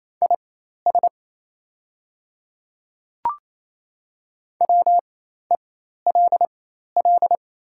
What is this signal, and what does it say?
Morse code, heard as a keyed tone, about 27 wpm, spaced as for 13 wpm, 690 Hz. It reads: IS WELL
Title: 4x Character Spacing - Top 200 Words - 28wpm